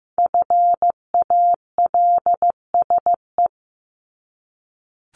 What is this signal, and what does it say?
Morse code, heard as a keyed tone, about 15 wpm, 700 Hz. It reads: FALSE
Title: Top 1501-1750 Words 15wpm